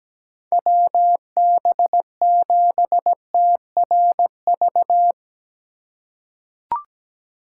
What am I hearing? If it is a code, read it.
Morse code, heard as a keyed tone, about 17 wpm, 700 Hz. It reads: WB7TRV
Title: USA Call Signs - 6 Characters - 2x Character Spacing - 17wpm